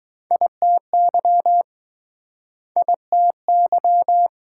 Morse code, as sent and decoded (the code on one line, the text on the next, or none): .. - -.-- / .. - -.--
ITY ITY